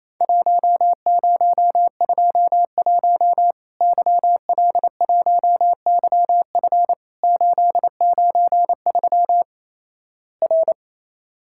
Morse code, heard as a keyed tone, about 28 wpm, 700 Hz. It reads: 1021 YL1YF 893 R
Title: Worked All Europe DX Contest - QTC Exchange - 28wpm